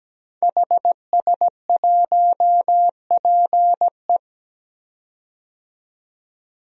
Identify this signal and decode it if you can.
Morse code, heard as a keyed tone, about 17 wpm, 700 Hz. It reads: HS1PE
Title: DXCC-100 Call Signs - 5 Characters - 17wpm